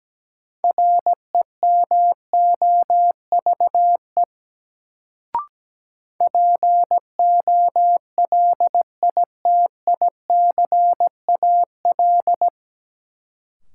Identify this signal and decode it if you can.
Morse code, heard as a keyed tone, about 17 wpm, 700 Hz. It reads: REMOVE POLITICAL
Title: Sentences from Top 800 Words - New Word Drill - 17wpm